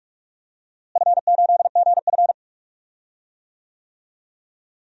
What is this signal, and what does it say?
Morse code, heard as a keyed tone, about 42 wpm, 700 Hz. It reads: F8GF
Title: DXCC-100 Call Signs - 4 Characters - Repeated 2x - 45wpm